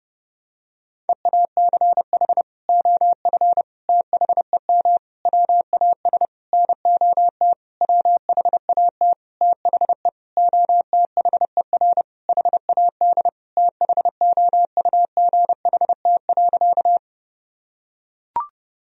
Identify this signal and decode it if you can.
Morse code, heard as a keyed tone, about 30 wpm, 700 Hz. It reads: EACH OF THEM WAS NOT WHAT THE OTHER HAD THOUGHT.
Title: Sentences from Top 1000 Words - Review Part 1 - 30wpm